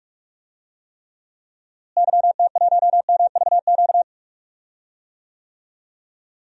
Morse code, heard as a keyed tone, about 46 wpm, 700 Hz. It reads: YT1MVQ
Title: DXCC-100 Call Signs - 6 Characters - Repeated 3x - 45wpm